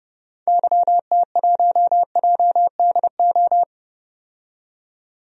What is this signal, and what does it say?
Morse code, heard as a keyed tone, about 30 wpm, 700 Hz. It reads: YT1JDO
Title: DXCC-100 Call Signs - 6 Characters - 30wpm